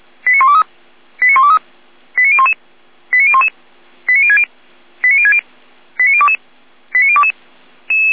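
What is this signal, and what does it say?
Five-tone selective call (ZVEI), heard as quick runs of five tones, tones of about 80 ms. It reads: 89123 89123 89011 89011 89060 89060 89022 89022
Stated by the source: ZVEI five-tone selective call received by SDR.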